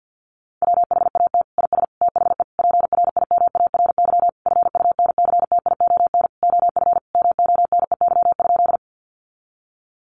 Morse code, heard as a keyed tone, about 48 wpm, 700 Hz. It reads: WHAT IS THE PRIMARY FUNCTION OF MONEY?
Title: Sentences from Top 1100 Words 50wpm